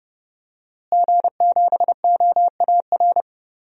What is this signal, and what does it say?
Morse code, heard as a keyed tone, about 30 wpm, 700 Hz. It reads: G7OAR
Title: DXCC-100 Call Signs - 5 Characters - Repeated 3x - 30wpm